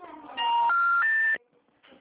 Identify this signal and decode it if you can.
Special information tone.